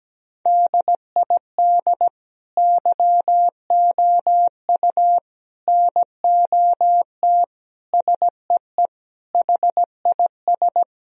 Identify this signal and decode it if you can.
Morse code, heard as a keyed tone, about 17 wpm, 700 Hz. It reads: DID YOU NOT SEE HIS
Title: Sentences from Top 1000 Words - Review Part 2 - 17wpm